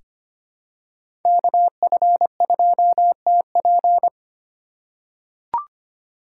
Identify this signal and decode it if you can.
Morse code, heard as a keyed tone, about 25 wpm, 700 Hz. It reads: KF2TP
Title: K1USN SST QSO Practice - 25wpm